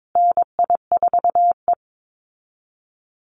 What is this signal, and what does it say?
Morse code, heard as a keyed tone, about 22 wpm, 700 Hz. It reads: NI4E
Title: Call Signs Repeated 2x - 22wpm